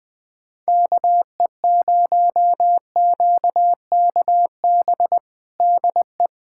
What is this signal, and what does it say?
Morse code, heard as a keyed tone, about 20 wpm, 700 Hz. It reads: KE0QKB DE
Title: USA Callsign DE Callsign - 6 Characters - 20wpm